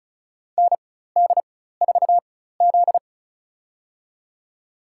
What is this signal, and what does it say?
Morse code, heard as a keyed tone, about 36 wpm, spaced as for 16 wpm, 700 Hz. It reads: ND4Z
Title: USA Call Signs - 4 Characters - 4x Character Spacing - 35wpm